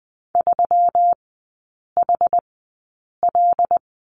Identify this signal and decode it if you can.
Morse code, heard as a keyed tone, about 20 wpm, spaced as for 8 wpm, 690 Hz. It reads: 3HL
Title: 133 Lesson 22e - Question Mark - Three Characters 20wpm